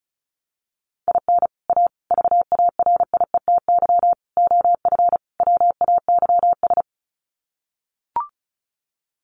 Morse code, heard as a keyed tone, about 34 wpm, 700 Hz. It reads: IN A VARIETY OF WAYS
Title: Sets of 5 Words Encore 35wpm